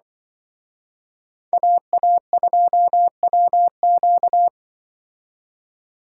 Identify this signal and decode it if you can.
Morse code, heard as a keyed tone, about 24 wpm, 700 Hz. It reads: AA2WQ
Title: Speed Racing - USA 2x2 Call Signs 20wpm